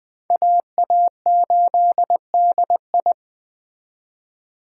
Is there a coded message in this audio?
Yes: Morse code from 0.3 to 3.1 s, heard as a keyed tone, about 20 wpm, 700 Hz. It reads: AA8DI